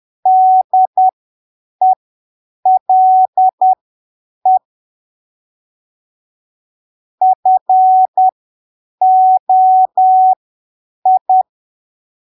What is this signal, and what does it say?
Morse code, heard as a keyed tone, about 10 wpm, spaced as for 7 wpm, 750 Hz. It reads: DELE FOI